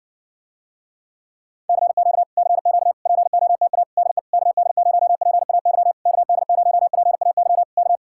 Morse code, heard as a keyed tone, about 60 wpm, 700 Hz. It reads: CQ CQ POTA DE KD0PAY KD0PAY K